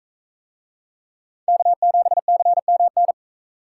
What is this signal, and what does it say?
Morse code, heard as a keyed tone, about 38 wpm, 700 Hz. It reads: K7CMN